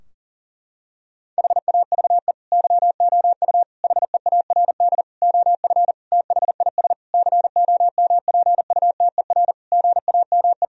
Morse code, heard as a keyed tone, about 40 wpm, 700 Hz. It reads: HAVE YOU HEARD OF THIS COMPUTER GAME
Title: Sentences from Top 1000 Words - Review Part 3 - 40wpm